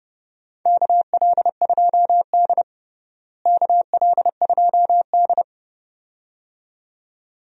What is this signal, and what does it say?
Morse code, heard as a keyed tone, about 30 wpm, 700 Hz. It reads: KL2D KL2D
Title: USA Call Signs - 4 Characters - Repeated 2x - 30wpm